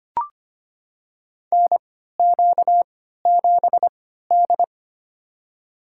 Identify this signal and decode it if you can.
Morse code, heard as a keyed tone, about 25 wpm, spaced as for 14 wpm, 700 Hz. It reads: NQ7D
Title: USA Call Signs - 4 Characters - 3x Character Spacing - 25wpm